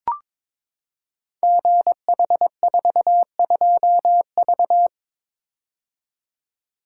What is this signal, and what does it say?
Morse code, heard as a keyed tone, about 22 wpm, 700 Hz. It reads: GH42V
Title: Code Groups - 5 Letter:Numbers - 22wpm